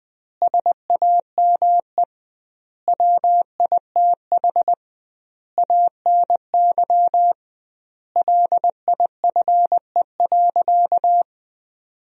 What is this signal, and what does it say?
Morse code, heard as a keyed tone, about 20 wpm, 700 Hz. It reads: SAME WITH ANY LIFE.
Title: 2x Word-Spacing - Sentences from Top 300 Words - 20wpm